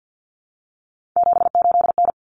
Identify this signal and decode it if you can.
Morse code, heard as a keyed tone, about 50 wpm, 700 Hz. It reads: 78N
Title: Common Cut Numbers - 3-Digit 50wpm